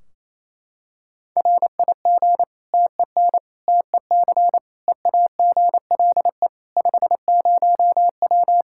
Morse code, heard as a keyed tone, about 28 wpm, 700 Hz. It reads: RIG TEN TEC EAGLE 50W
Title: Practice QSO - Exchange 2 - Info - 28wpm